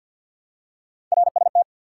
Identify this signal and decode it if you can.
Morse code, heard as a keyed tone, about 53 wpm, 700 Hz. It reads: RST
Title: Top 90 QSO Elements 50wpm